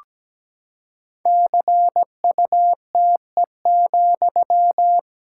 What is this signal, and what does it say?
Morse code, heard as a keyed tone, about 17 wpm, 700 Hz. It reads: CUTE,
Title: Pangram Challenge v2 - 17wpm